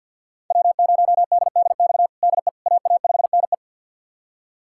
Morse code, heard as a keyed tone, about 53 wpm, 700 Hz. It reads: W0DDX DE AA5NE
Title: USA Callsign DE Callsign - 5 Characters - 50wpm